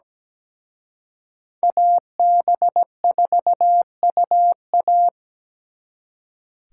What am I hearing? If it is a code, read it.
Morse code, heard as a keyed tone, about 20 wpm, 700 Hz. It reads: AB4UA
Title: Speed Racing - USA 2x2 Call Signs 17wpm